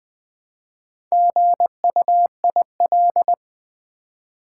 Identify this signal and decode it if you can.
Morse code, heard as a keyed tone, about 20 wpm, 700 Hz. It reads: GUIL
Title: Build A Word - 1 Letter At A Time - 3 to 5-Letter Words - 20wpm